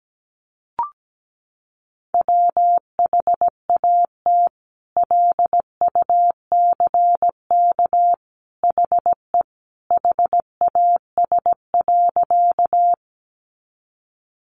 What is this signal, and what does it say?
Morse code, heard as a keyed tone, about 17 wpm, 700 Hz. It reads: WHAT LUCK HE HAS.
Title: Sentences from Top 1500 Words 17wpm